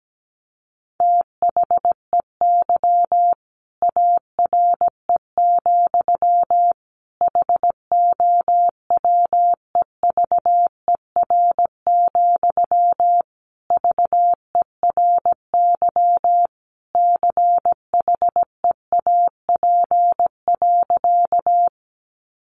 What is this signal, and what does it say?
Morse code, heard as a keyed tone, about 17 wpm, 700 Hz. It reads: THEY ARE, HOWEVER, VERY CHEAP.
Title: Sentences from Top 1200 Words 17wpm